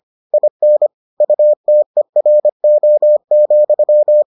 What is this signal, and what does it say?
Morse code, heard as a keyed tone, about 25 wpm, 600 Hz. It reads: IN UTERO,